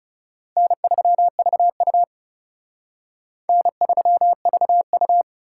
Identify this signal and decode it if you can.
Morse code, heard as a keyed tone, about 34 wpm, 700 Hz. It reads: N3VU N3VU
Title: Speed Racing - USA 1x2 Call Signs 30wpm